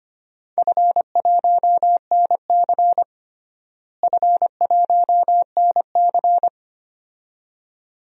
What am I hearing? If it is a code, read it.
Morse code, heard as a keyed tone, about 25 wpm, 700 Hz. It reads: F1NC F1NC
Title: DXCC-100 Call Signs - 4 Characters - Repeated 2x - 25wpm